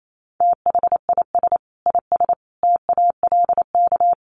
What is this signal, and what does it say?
Morse code, heard as a keyed tone, about 28 wpm, 700 Hz. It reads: THIS IS TALK
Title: Sets of 5 Words 28wpm